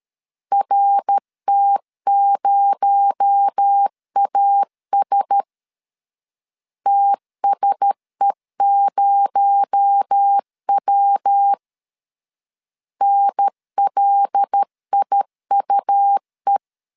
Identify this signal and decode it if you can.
Morse code, heard as a keyed tone, about 13 wpm, 800 Hz. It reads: RT0AS TSE0W NLIUE
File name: LEZIONI / 6 PKX0 / 6 PKX0 12wpm 2231